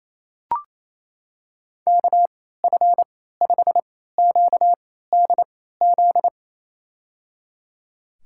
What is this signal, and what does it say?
Morse code, heard as a keyed tone, about 27 wpm, spaced as for 16 wpm, 700 Hz. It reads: KF5QDZ
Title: USA Call Signs - 6 Characters - 3x Character Spacing - 28wpm